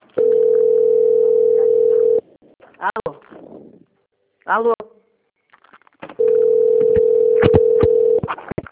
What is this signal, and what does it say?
Ringback tone (North America), 440 + 480 Hz, with speech.
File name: out- - - - - .24159